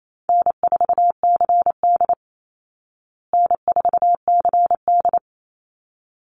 Morse code, heard as a keyed tone, about 28 wpm, 700 Hz. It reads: N4CD N4CD
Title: Easy Call Signs Repeated 2x - 28wpm